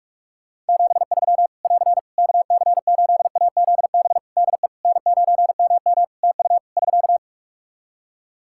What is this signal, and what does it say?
Morse code, heard as a keyed tone, about 46 wpm, 700 Hz. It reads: 73 <AR> KC8AZB DE N9MM TU <SK>
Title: K1USN SST QSO Practice 45wpm